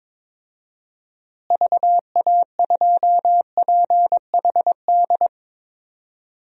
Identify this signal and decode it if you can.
Morse code, heard as a keyed tone, about 22 wpm, 700 Hz. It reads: VA2PHD
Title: DXCC-100 Call Signs - 6 Characters - Repeated 2x - 22wpm